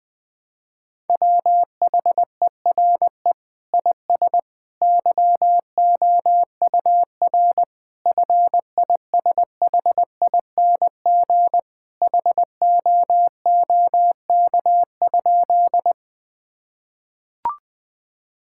Morse code, heard as a keyed tone, about 21 wpm, 700 Hz. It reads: WHERE IS YOUR FISHING HOOK?
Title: Sentences from Top 1600 Words 20wpm